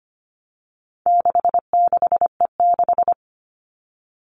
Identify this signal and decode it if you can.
Morse code, heard as a keyed tone, about 25 wpm, 700 Hz. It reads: 66E6
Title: All Cut Numbers - 4-Digit 25wpm